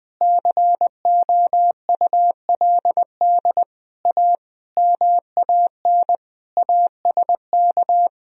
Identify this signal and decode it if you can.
Morse code, heard as a keyed tone, about 20 wpm, 700 Hz. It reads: COULD A MAN ASK
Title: Sentences from Top 200 Words and Abbreviations 20wpm